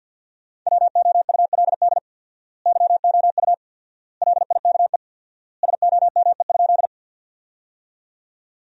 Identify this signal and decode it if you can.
Morse code, heard as a keyed tone, about 47 wpm, 700 Hz. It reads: WOULD YOU LIKE SOME?
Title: 4x Word-Spacing - Sentences from Top 100 Words - 50wpm